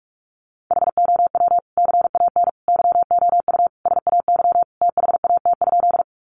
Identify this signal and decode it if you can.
Morse code, heard as a keyed tone, about 44 wpm, 700 Hz. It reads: HOW CAN YOU SAY THAT?
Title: Sentences from Top 200 Words 45wpm